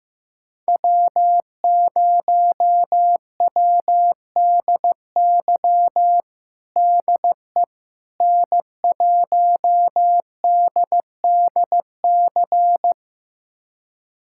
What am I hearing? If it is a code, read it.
Morse code, heard as a keyed tone, about 15 wpm, 700 Hz. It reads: W0WDY DE N1DDC
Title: USA Callsign DE Callsign - 5 Characters - 15wpm